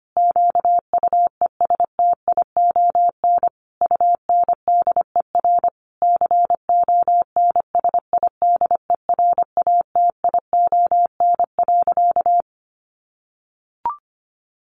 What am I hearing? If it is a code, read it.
Morse code, heard as a keyed tone, about 25 wpm, 700 Hz. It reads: QUESTION UNDER CONSIDERATION.
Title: Sentences from Top 1500 Words 25wpm